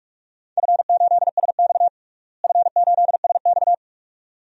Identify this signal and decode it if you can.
Morse code, heard as a keyed tone, about 42 wpm, 700 Hz. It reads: F8SX F8SX
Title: DXCC-100 Call Signs - 4 Characters - Repeated 2x - 45wpm